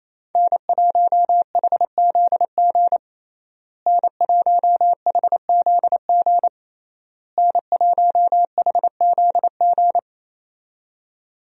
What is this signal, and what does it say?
Morse code, heard as a keyed tone, about 28 wpm, 700 Hz. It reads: N1HZG N1HZG N1HZG
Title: USA Call Signs - 5 Characters - Repeated 3x - 28wpm